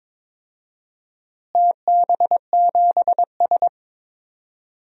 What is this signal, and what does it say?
Morse code, heard as a keyed tone, about 22 wpm, 700 Hz. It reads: TB7S
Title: DXCC-100 Call Signs - 4 Characters - Repeated 3x - 22wpm